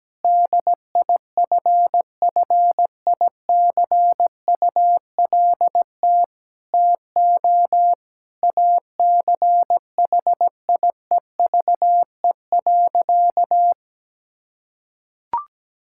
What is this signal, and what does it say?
Morse code, heard as a keyed tone, about 17 wpm, 700 Hz. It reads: DIFFICULT TO ACHIEVE.
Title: Sentences from Top 1000 Words - Review Part 6 - 17wpm